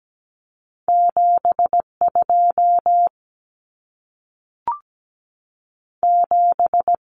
Morse code, heard as a keyed tone, about 17 wpm, 700 Hz. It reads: 7 2 7
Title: Numbers - 2-Digit 17wpm